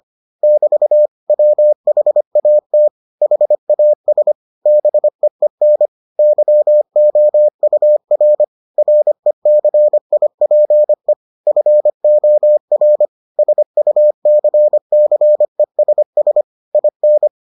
Morse code, heard as a keyed tone, about 25 wpm, 600 Hz. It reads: <BT> WHAT HAS BEEN YOUR RECIPE FOR SUCCESS IN